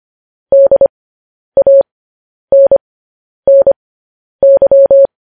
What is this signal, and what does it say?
Morse code, heard as a keyed tone, about 25 wpm, spaced as for 10 wpm, 550 Hz. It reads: DANNY
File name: podcast / News headlines-1x-25x10